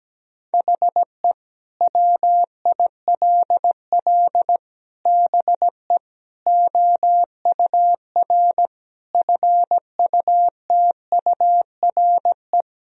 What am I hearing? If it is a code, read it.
Morse code, heard as a keyed tone, about 17 wpm, 700 Hz. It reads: HE WILL BE OUR FUTURE
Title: Sentences from Top 1000 Words - Review Part 3 - 17wpm